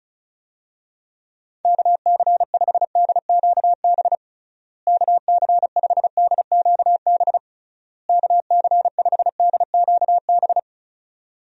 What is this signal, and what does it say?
Morse code, heard as a keyed tone, about 34 wpm, 700 Hz. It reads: KC5DQB KC5DQB KC5DQB
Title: USA Call Signs - 6 Characters - Repeated 3x - 35wpm